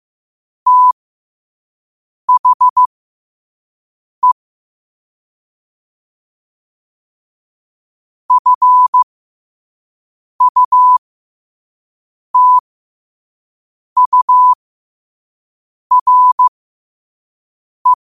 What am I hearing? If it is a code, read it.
Morse code, heard as a keyed tone, about 15 wpm, spaced as for 5 wpm, 1000 Hz. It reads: THE FUTURE
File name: qotdfiles / 5wpm